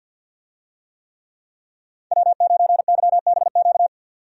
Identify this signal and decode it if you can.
Morse code, heard as a keyed tone, about 51 wpm, 700 Hz. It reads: W9YBQ
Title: USA Call Signs - 5 Characters - 50wpm